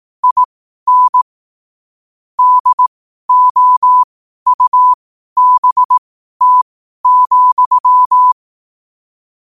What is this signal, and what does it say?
Morse code, heard as a keyed tone, about 18 wpm, spaced as for 12 wpm, 1000 Hz. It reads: IN DOUBT,
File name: qotdfiles / 12wpm